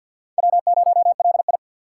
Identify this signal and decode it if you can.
Morse code, heard as a keyed tone, about 54 wpm, 700 Hz. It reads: W0LI
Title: USA Call Signs - 4 Characters - 6x Character Spacing - 50wpm